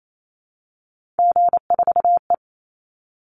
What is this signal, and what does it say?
Morse code, heard as a keyed tone, about 28 wpm, 700 Hz. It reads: G4E
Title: Call Signs Repeated 3x - 28wpm